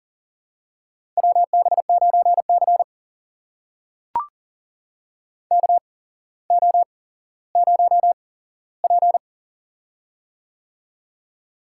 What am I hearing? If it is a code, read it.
Morse code, heard as a keyed tone, about 39 wpm, spaced as for 17 wpm, 700 Hz. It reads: WB9C KO0P
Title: USA Call Signs - 4 Characters - 8x Character Spacing - 40wpm